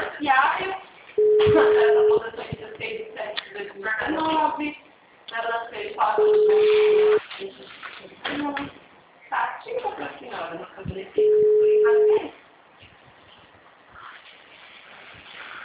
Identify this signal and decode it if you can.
Ringback tone (Europe, 425 Hz), with speech.